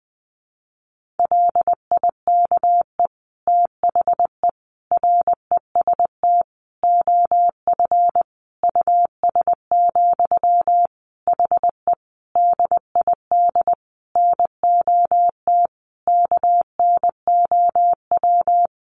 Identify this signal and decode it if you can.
Morse code, heard as a keyed tone, about 19 wpm, 700 Hz. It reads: LIKE THE REST OF US, HE DID NOT KNOW
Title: Sentences from Top 1400 Words 20wpm